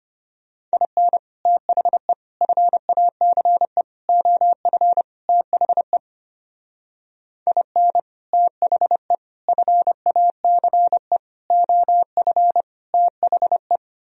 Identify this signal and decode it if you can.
Morse code, heard as a keyed tone, about 30 wpm, 700 Hz. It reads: IN THE FACE OF THE IN THE FACE OF THE
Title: Speed-Racing - Sets of 5 Words 25wpm